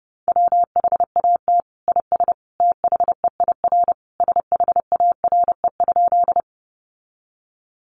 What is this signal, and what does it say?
Morse code, heard as a keyed tone, about 29 wpm, 700 Hz. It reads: WHAT IS THEIR SHARE?
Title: Sentences from Top 900 Words 30wpm